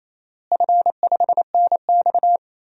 Morse code, heard as a keyed tone, about 28 wpm, 700 Hz. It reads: F5NX
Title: DXCC-100 Call Signs - 4 Characters - Repeated 2x - 28wpm